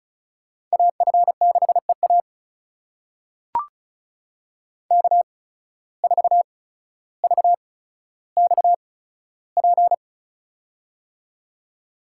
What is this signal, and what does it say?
Morse code, heard as a keyed tone, about 36 wpm, spaced as for 15 wpm, 700 Hz. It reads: AF6EA K4VXP
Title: USA Call Signs - 5 Characters - 8x Character Spacing - 35wpm